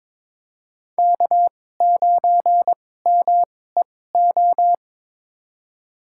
Morse code, heard as a keyed tone, about 22 wpm, spaced as for 16 wpm, 700 Hz. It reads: K9MEO
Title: USA Call Signs - 5 Characters - 2x Character Spacing - 22wpm